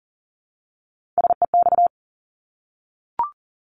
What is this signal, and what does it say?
Morse code, heard as a keyed tone, about 38 wpm, 700 Hz. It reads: SEX
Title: Multi-letter Phonemes Repeated 3x 40wpm